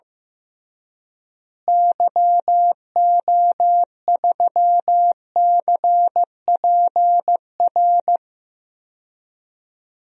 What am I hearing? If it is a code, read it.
Morse code, heard as a keyed tone, about 15 wpm, 700 Hz. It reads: YO3CPR
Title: DXCC-100 Call Signs - 6 Characters - Repeated 2x - 15wpm